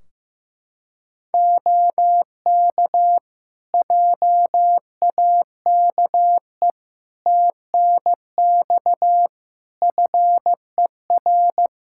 Morse code, heard as a keyed tone, about 15 wpm, 700 Hz. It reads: OK JAKE TNX FER
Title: Practice QSO - Exchange 4 - Ending - 15wpm